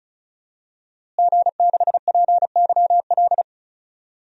Morse code, heard as a keyed tone, about 36 wpm, 700 Hz. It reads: G6PYL
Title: DXCC-100 Call Signs - 5 Characters - Repeated 2x - 35wpm